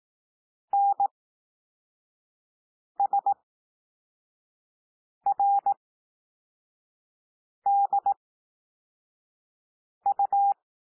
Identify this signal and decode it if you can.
Morse code, heard as a keyed tone, about 18 wpm, spaced as for 4 wpm, 800 Hz. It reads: NSRDU
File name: cwmp3 / L2 SOUND LETTERS 18 wpm